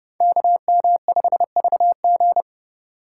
Morse code, heard as a keyed tone, about 29 wpm, 700 Hz. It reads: KM5VG